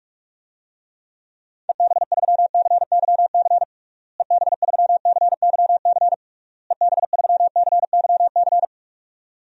Morse code, heard as a keyed tone, about 46 wpm, 700 Hz. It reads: EB3CYC EB3CYC EB3CYC
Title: DXCC-100 Call Signs - 6 Characters - Repeated 3x - 45wpm